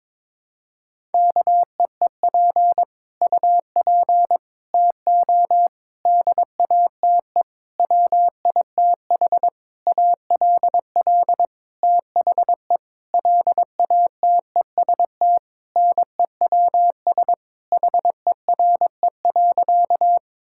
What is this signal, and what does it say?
Morse code, heard as a keyed tone, about 22 wpm, 700 Hz. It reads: KEEP UP TO DATE WITH ALL THE LATEST NEWS HERE.
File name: Sentences with 9-Word Phrases - 22wpm